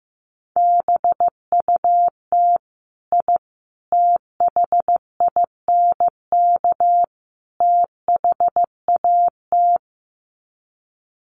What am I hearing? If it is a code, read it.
Morse code, heard as a keyed tone, about 15 wpm, 700 Hz. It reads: BUT I THINK THAT
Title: Sets of 4 Words 15wpm